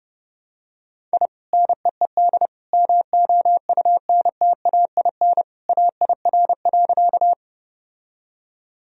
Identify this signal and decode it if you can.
Morse code, heard as a keyed tone, about 30 wpm, 700 Hz. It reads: I NEED MOUNTAIN AIR.
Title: Sentences from Top 1000 Words - Review Part 1 - 30wpm